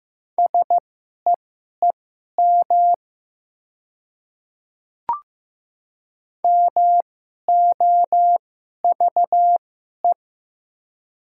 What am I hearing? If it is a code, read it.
Morse code, heard as a keyed tone, about 15 wpm, spaced as for 11 wpm, 690 Hz. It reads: SEEM MOVE
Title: 2x Character Spacing - Top 300 Words - 15wpm